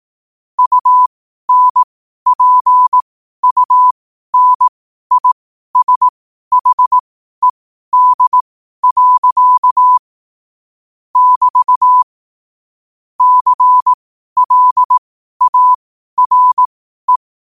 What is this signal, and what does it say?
Morse code, heard as a keyed tone, about 18 wpm, spaced as for 12 wpm, 1000 Hz. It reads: UNPUNISHED. <BT> CLARE
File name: qotdfiles / 12wpm